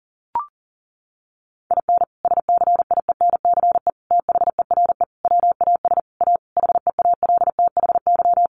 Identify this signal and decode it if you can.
Morse code, heard as a keyed tone, about 40 wpm, 700 Hz. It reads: IN SCIENCE THERE WAS A HEALTHY
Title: Sentences from Top 1400 Words 40wpm